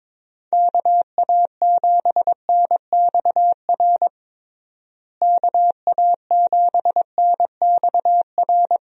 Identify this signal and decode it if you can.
Morse code, heard as a keyed tone, about 22 wpm, 700 Hz. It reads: KA7NXR KA7NXR
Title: USA Call Signs - 6 Characters - Repeated 2x - 22wpm